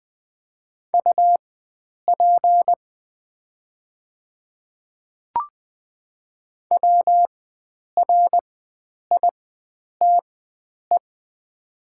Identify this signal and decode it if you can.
Morse code, heard as a keyed tone, about 20 wpm, spaced as for 9 wpm, 700 Hz. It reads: UP WRITE
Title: 4x Character Spacing - Top 100 Words - 20wpm